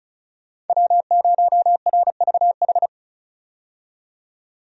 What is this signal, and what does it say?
Morse code, heard as a keyed tone, about 36 wpm, 700 Hz. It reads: W0RVH
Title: USA Call Signs - 5 Characters - 35wpm